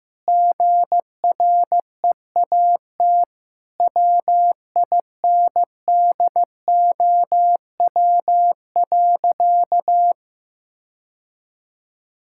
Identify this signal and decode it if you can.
Morse code, heard as a keyed tone, about 15 wpm, 700 Hz. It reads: GREAT WINDOW.
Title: Sentences from Top 1600 Words 15wpm